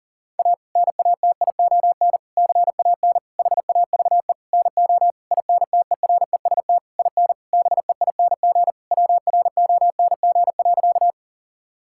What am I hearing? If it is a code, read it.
Morse code, heard as a keyed tone, about 40 wpm, 700 Hz. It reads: A NATION CAN HAVE NO INTEREST IN BEING WRONG.
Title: Sentences from Top 1000 Words - Review Part 6 - 40wpm